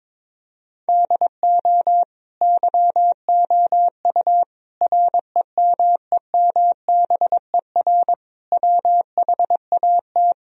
Morse code, heard as a keyed tone, about 22 wpm, 700 Hz. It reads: DO YOU REMEMBER WHAT
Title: Sentences from Top 1000 Words - Review Part 4 - 22wpm